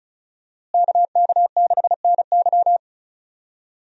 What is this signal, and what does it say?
Morse code, heard as a keyed tone, about 34 wpm, 700 Hz. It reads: KK6NY
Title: USA Call Signs - 5 Characters - Repeated 2x - 35wpm